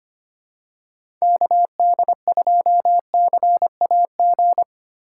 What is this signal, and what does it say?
Morse code, heard as a keyed tone, about 25 wpm, 700 Hz. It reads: KD2CAG
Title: USA Call Signs - 6 Characters - 4x Character Spacing - 25wpm